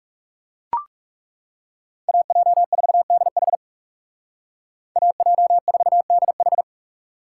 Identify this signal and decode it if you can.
Morse code, heard as a keyed tone, about 39 wpm, 700 Hz. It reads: AJ4DH AJ4DH